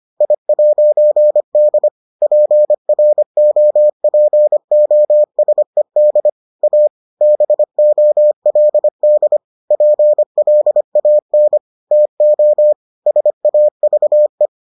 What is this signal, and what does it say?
Morse code, heard as a keyed tone, about 25 wpm, 600 Hz. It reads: I'D PROPOSED A BOLD PLAN TO SAVE